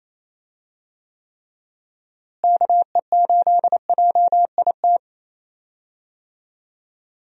Morse code, heard as a keyed tone, about 27 wpm, 700 Hz. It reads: KE8JIT
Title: USA Call Signs - 6 Characters - Repeated 2x - 28wpm